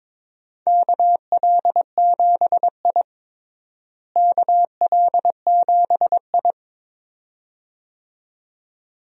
Morse code, heard as a keyed tone, about 22 wpm, 700 Hz. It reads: KL7I KL7I
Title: DXCC-100 Call Signs - 4 Characters - Repeated 2x - 22wpm